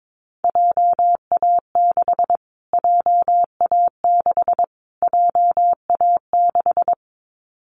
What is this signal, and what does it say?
Morse code, heard as a keyed tone, about 22 wpm, 700 Hz. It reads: JA6 JA6 JA6
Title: Callsign Prefixes for Top 25 DXCC - Repeated 3x - 22wpm